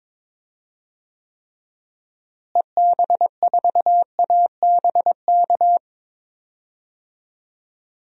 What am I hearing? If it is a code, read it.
Morse code, heard as a keyed tone, about 22 wpm, 700 Hz. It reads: EB4ABK
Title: DXCC-100 Call Signs - 6 Characters - Repeated 3x - 22wpm